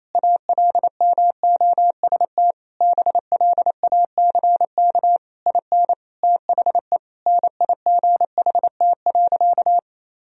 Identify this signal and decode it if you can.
Morse code, heard as a keyed tone, about 28 wpm, 700 Hz. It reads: ALMOST BLACK IN THE NIGHT.
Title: Sentences from Top 1000 Words - Review Part 4 - 28wpm